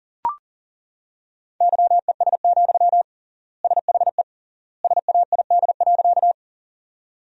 Morse code, heard as a keyed tone, about 40 wpm, 700 Hz. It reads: YES, SHE SAID.
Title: 3x Word-Spacing - Sentences from Top 500 Words - 40wpm